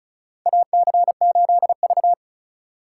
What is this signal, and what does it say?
Morse code, heard as a keyed tone, about 36 wpm, 700 Hz. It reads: AC8V
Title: USA Call Signs - 2x Character Spacing - 35wpm